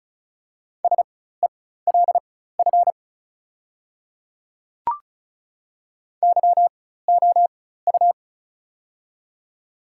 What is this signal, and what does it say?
Morse code, heard as a keyed tone, about 36 wpm, spaced as for 16 wpm, 700 Hz. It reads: SELF YOU
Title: 4x Character Spacing - Top 300 Words - 35wpm